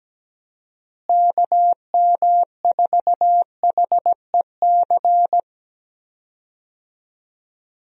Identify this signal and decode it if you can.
Morse code, heard as a keyed tone, about 17 wpm, 700 Hz. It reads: KM4HEC
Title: USA Call Signs - 6 Characters - Repeated 2x - 17wpm